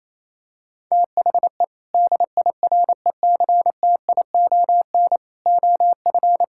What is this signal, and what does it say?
Morse code, heard as a keyed tone, about 28 wpm, 700 Hz. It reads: THE DIRECTION OF
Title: Sentences from Top 1000 Words - Review Part 5 - 28wpm